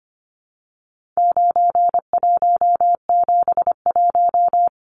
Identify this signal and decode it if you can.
Morse code, heard as a keyed tone, about 25 wpm, 700 Hz. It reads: 9171
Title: Numbers - 4-Digit 25wpm